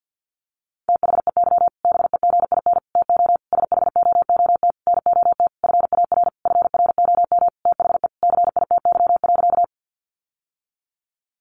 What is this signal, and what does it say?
Morse code, heard as a keyed tone, about 50 wpm, 700 Hz. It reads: THEY BEGIN TO SHOOT NOT FAR FROM THE CITY.
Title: Sentences from Top 1200 Words 50wpm